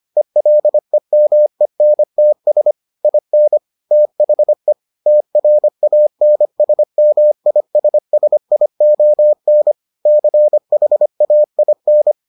Morse code, heard as a keyed tone, about 25 wpm, 600 Hz. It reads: ELEMENTS IN THE TRANSMISSION CHAIN